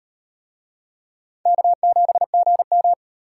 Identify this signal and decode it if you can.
Morse code, heard as a keyed tone, about 38 wpm, 700 Hz. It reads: K7GM